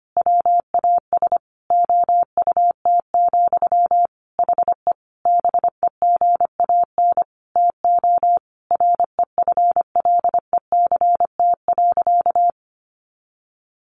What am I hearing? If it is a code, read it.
Morse code, heard as a keyed tone, about 25 wpm, 700 Hz. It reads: WAS OUT, HE BEGAN TO REFLECT.
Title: Sentences from Top 1500 Words 25wpm